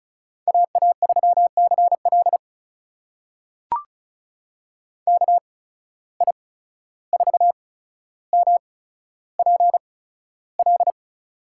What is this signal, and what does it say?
Morse code, heard as a keyed tone, about 36 wpm, spaced as for 14 wpm, 700 Hz. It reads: AA3CL KI4MPL
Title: USA Call Signs - 8x Character Spacing - 35wpm